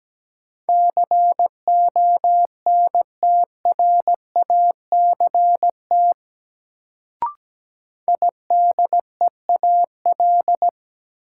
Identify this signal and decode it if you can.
Morse code, heard as a keyed tone, about 17 wpm, 700 Hz. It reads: CONTRACT IDEAL
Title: Sentences from Top 1400 Words - New Word Drill - 17wpm